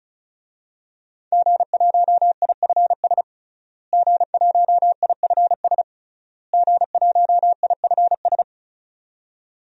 Morse code, heard as a keyed tone, about 34 wpm, 700 Hz. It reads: G1IFS G1IFS G1IFS
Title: DXCC-100 Call Signs - 5 Characters - Repeated 3x - 35wpm